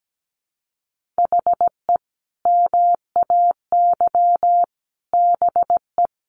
Morse code, heard as a keyed tone, about 17 wpm, 700 Hz. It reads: HE MAY BE
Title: Sentences from Top 1400 Words 17wpm